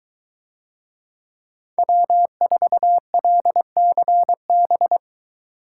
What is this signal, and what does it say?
Morse code, heard as a keyed tone, about 21 wpm, 700 Hz. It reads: W4LCB